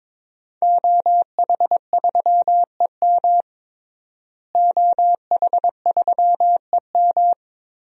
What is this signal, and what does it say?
Morse code, heard as a keyed tone, about 22 wpm, 700 Hz. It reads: OH3EM OH3EM
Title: DXCC-100 Call Signs - 5 Characters - Repeated 3x - 22wpm